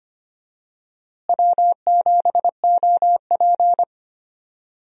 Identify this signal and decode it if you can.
Morse code, heard as a keyed tone, about 25 wpm, 700 Hz. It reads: W7OP
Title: USA Call Signs - 4 Characters - Repeated 2x - 25wpm